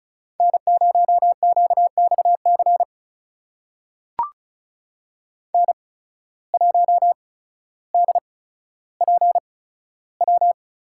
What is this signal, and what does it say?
Morse code, heard as a keyed tone, about 36 wpm, spaced as for 15 wpm, 700 Hz. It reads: N0QXC N1DPW
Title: USA Call Signs - 5 Characters - 8x Character Spacing - 35wpm